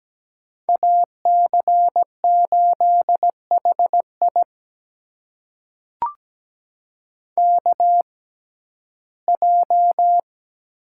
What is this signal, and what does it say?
Morse code, heard as a keyed tone, about 17 wpm, spaced as for 12 wpm, 700 Hz. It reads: AC8HI KJ
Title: USA Call Signs - 6x Character Spacing - 17wpm